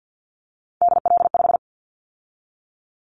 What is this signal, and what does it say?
Morse code, heard as a keyed tone, about 50 wpm, 700 Hz. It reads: DL5
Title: Callsign Prefixes for Top 50 DXCC - 50wpm